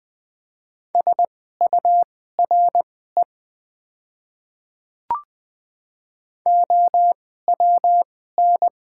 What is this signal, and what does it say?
Morse code, heard as a keyed tone, about 20 wpm, spaced as for 14 wpm, 700 Hz. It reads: SURE OWN
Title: 2x Character Spacing - Top 300 Words - 20wpm